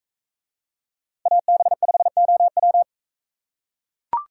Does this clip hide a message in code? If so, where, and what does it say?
Yes: Morse code from 1.3 to 2.8 s, heard as a keyed tone, about 43 wpm, 700 Hz. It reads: AB5OW